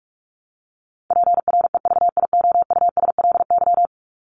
Morse code, heard as a keyed tone, about 44 wpm, 700 Hz. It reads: PREVIOUSLY
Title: LY Suffix - Top 100 Words with Base Words 45wpm